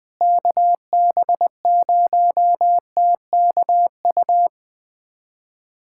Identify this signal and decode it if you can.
Morse code, heard as a keyed tone, about 20 wpm, 700 Hz. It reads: KB0TKU
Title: USA Callsign DE Callsign - 6 Characters - 20wpm